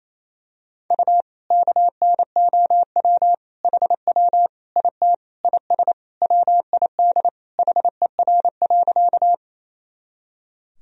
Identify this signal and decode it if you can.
Morse code, heard as a keyed tone, about 28 wpm, 700 Hz. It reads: U KNOW HW IT IS WID HER.
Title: Sentences from Top 100 Words and Abbreviations 28wpm